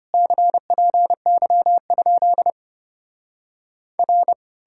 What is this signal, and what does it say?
Morse code, heard as a keyed tone, about 33 wpm, 700 Hz. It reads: CPY? R